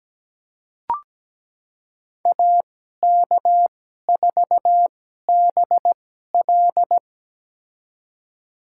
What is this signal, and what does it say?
Morse code, heard as a keyed tone, about 17 wpm, spaced as for 12 wpm, 700 Hz. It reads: AK4BL
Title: USA Call Signs - 2x Character Spacing - 17wpm